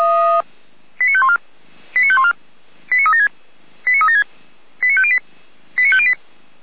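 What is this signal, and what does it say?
Five-tone selective call (ZVEI), heard as quick runs of five tones, tones of about 80 ms. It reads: 89514 89514 89376 89376 89508 89508
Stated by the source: ZVEI five-tone selective call 89115 received on an SDR.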